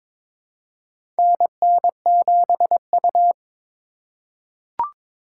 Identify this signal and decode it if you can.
Morse code, heard as a keyed tone, about 21 wpm, 700 Hz. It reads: NN7U
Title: USA Call Signs - 4 Characters - 4x Character Spacing - 22wpm